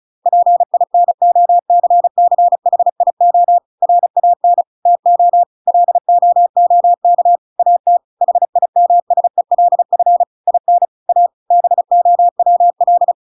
Morse code, heard as a keyed tone, about 34 wpm, 700 Hz. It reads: PINOCCHIO RAN TO LOOK AT HIMSELF IN A BOWL
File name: Pinocchio / Pinocchio ch8 35 35 ik1hge